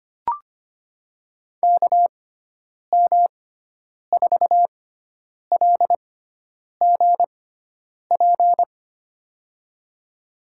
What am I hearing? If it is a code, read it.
Morse code, heard as a keyed tone, about 25 wpm, spaced as for 9 wpm, 700 Hz. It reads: KM4LGP
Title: USA Call Signs - 6 Characters - 6x Character Spacing - 25wpm